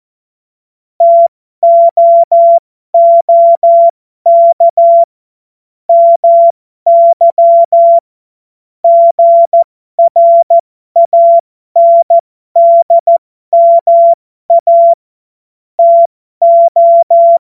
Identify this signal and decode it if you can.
Morse code, heard as a keyed tone, about 14 wpm, spaced as for 12 wpm, 680 Hz. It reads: TOOK MY GRANDMA TO
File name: Morsekurs / Witze